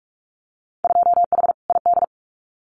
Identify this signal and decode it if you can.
Morse code, heard as a keyed tone, about 44 wpm, 700 Hz. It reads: 2H ID
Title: Winter Field Day Exchanges 45wpm